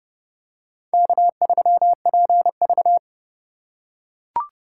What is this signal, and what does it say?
Morse code, heard as a keyed tone, about 30 wpm, 700 Hz. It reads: K3PV